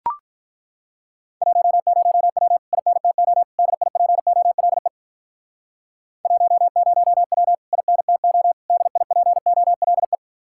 Morse code, heard as a keyed tone, about 42 wpm, 700 Hz. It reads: 10W INTO DIPOLE 10W INTO DIPOLE
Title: Speed-Racing Common QSO Phrases - 35wpm